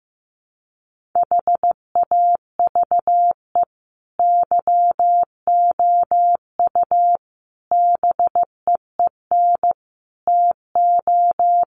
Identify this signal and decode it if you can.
Morse code, heard as a keyed tone, about 15 wpm, 700 Hz. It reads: HAVE YOU BEEN TO
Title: Sentences from Top 1500 Words 15wpm